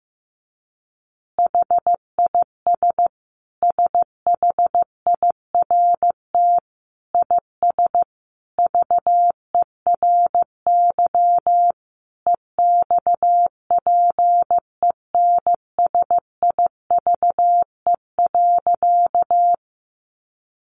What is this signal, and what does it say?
Morse code, heard as a keyed tone, about 15 wpm, 700 Hz. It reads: HIS SHIRT IS VERY EXPENSIVE.
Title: Additional Easy Phrases 15wpm